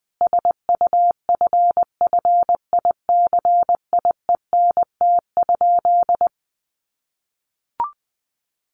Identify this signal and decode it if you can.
Morse code, heard as a keyed tone, about 20 wpm, 700 Hz. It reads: SUFFICIENT?
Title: Sentences from Top 1400 Words 20wpm